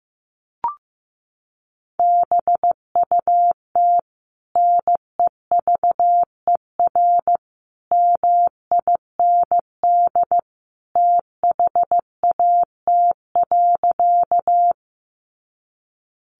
Morse code, heard as a keyed tone, about 15 wpm, 700 Hz. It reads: BUT NEVER MIND THAT.
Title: Sentences from Top 500 Words 15wpm